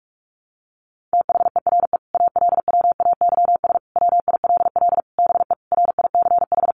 Morse code, heard as a keyed tone, about 44 wpm, 700 Hz. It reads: THERE ALWAYS WILL BE RICH
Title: Sentences from Top 1100 Words 45wpm